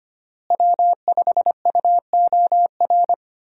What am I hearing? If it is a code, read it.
Morse code, heard as a keyed tone, about 25 wpm, 700 Hz. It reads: W5UOR
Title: USA Call Signs - 5 Characters - Repeated 3x - 25wpm